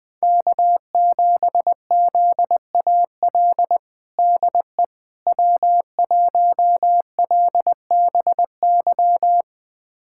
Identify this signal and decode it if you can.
Morse code, heard as a keyed tone, about 20 wpm, 700 Hz. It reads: K7ZAL DE W1LBY
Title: USA Callsign DE Callsign - 5 Characters - 20wpm